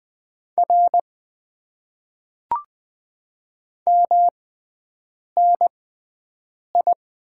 Morse code, heard as a keyed tone, about 20 wpm, spaced as for 7 wpm, 700 Hz. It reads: R MNI
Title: Top 90 QSO Elements - 6x Character Spacing - 20wpm